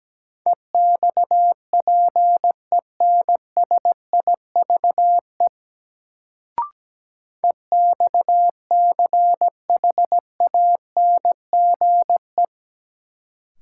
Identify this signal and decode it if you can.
Morse code, heard as a keyed tone, about 17 wpm, 700 Hz. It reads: EXPENSIVE EXCHANGE
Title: Sentences from Top 1000 Words - Review Part 6 - New Word Drill - 17wpm